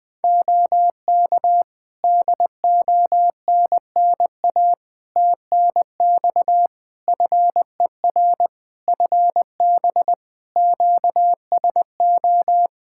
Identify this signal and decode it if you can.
Morse code, heard as a keyed tone, about 20 wpm, 700 Hz. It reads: OK DONNA TNX FER FB QSO
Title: Practice QSO - Exchange 4 - Ending - 20wpm